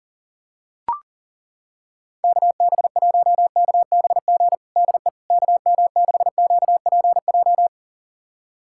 Morse code, heard as a keyed tone, about 40 wpm, 700 Hz. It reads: KB1KBG DE KM6QPJ
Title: USA Callsign DE Callsign 40wpm